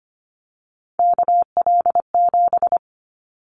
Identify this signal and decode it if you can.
Morse code, heard as a keyed tone, about 25 wpm, 700 Hz. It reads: KL7
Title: Callsign Prefixes for Top 100 DXCC - 25wpm